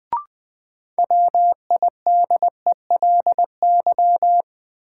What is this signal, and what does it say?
Morse code, heard as a keyed tone, about 20 wpm, 700 Hz. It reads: WIDELY
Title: Sentences from Top 1200 Words - New Word Drill Alt - 20wpm